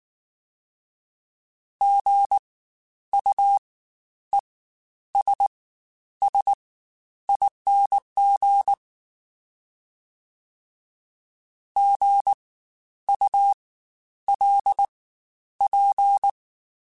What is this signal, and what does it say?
Morse code, heard as a keyed tone, about 19 wpm, spaced as for 10 wpm, 800 Hz. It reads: GUESSING GULP
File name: Spaced Alphas / Spaced Alphas mp3s / Gs sp @18